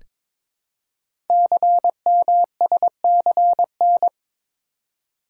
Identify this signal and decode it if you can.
Morse code, heard as a keyed tone, about 23 wpm, 690 Hz. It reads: CMSCN
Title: Code Groups - 5 Letters - 22wpm